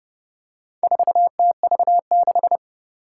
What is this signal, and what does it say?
Morse code, heard as a keyed tone, about 29 wpm, 700 Hz. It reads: 4TV6
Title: Four Challenging Letter-Number Combinations 30wpm